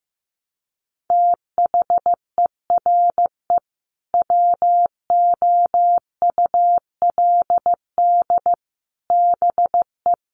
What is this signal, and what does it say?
Morse code, heard as a keyed tone, about 15 wpm, 700 Hz. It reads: THERE WOULD BE
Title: Sentences from Top 1200 Words 15wpm